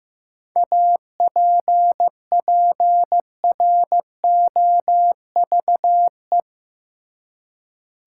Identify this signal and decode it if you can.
Morse code, heard as a keyed tone, about 16 wpm, 700 Hz. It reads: APPROVE
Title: Seven Letter Words - 15wpm